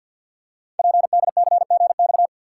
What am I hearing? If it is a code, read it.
Morse code, heard as a keyed tone, about 53 wpm, 700 Hz. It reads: PDCGX
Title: Code Groups - 5 Letters - 50wpm